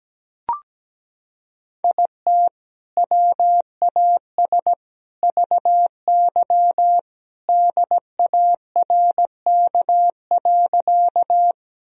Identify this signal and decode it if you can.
Morse code, heard as a keyed tone, about 17 wpm, 700 Hz. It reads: IT WAS VY DARK.
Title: Sentences from Top 500 Words and Abbreviations 17wpm